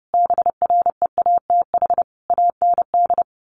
Morse code, heard as a keyed tone, about 30 wpm, 700 Hz. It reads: BREATH AND
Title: Sets of 5 Words Encore 30wpm